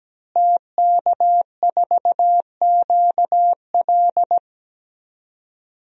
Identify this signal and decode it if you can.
Morse code, heard as a keyed tone, about 17 wpm, 700 Hz. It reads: TK4QL
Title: DXCC-100 Call Signs - 5 Characters - Repeated 3x - 17wpm